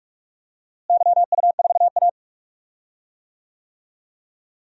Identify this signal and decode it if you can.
Morse code, heard as a keyed tone, about 48 wpm, 690 Hz. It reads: YU4A
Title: DXCC-100 Call Signs - 4 Characters - Repeated 3x - 45wpm